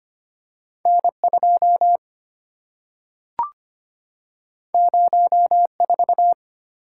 Morse code, heard as a keyed tone, about 25 wpm, 700 Hz. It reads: N2 04
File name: Two Letter-Number Combinations - 25wpm